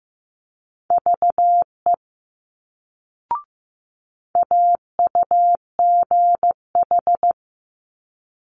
Morse code, heard as a keyed tone, about 15 wpm, 700 Hz. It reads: VE AUGH
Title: Multi-letter Phonemes Quick Fire 15wpm